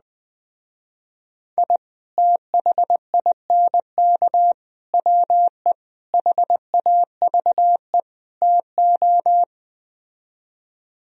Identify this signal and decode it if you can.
Morse code, heard as a keyed tone, about 20 wpm, 700 Hz. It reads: I THINK WE HAVE TO